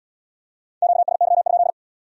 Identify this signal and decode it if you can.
Morse code, heard as a keyed tone, about 68 wpm, 700 Hz. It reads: QSY?